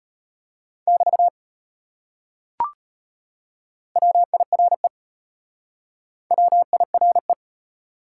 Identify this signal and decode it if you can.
Morse code, heard as a keyed tone, about 33 wpm, 700 Hz. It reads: <BT> WIRE WIRE